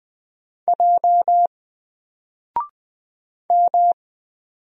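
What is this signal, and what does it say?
Morse code, heard as a keyed tone, about 20 wpm, 700 Hz. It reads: J M
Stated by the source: Single letter and number Morse drill at 20 wpm, from N4LCD.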